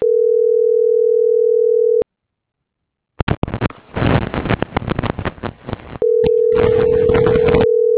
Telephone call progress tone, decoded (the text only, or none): ringback tone